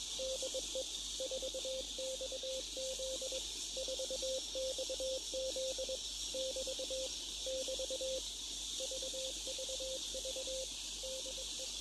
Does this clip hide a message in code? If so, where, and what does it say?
Yes: Morse code from 0.2 to 11.7 s, heard as a keyed tone, about 21 wpm, 520 Hz, with radio noise. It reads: DE 4XZ 4XZ <BT> <BT> VVV DE